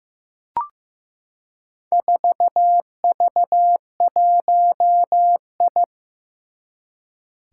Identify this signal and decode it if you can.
Morse code, heard as a keyed tone, about 15 wpm, 700 Hz. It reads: 4V1I
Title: Four Challenging Letter-Number Combinations 15wpm